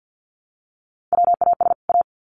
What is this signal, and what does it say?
Morse code, heard as a keyed tone, about 50 wpm, 700 Hz. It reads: WAS A
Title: Sets of 2 Words 50wpm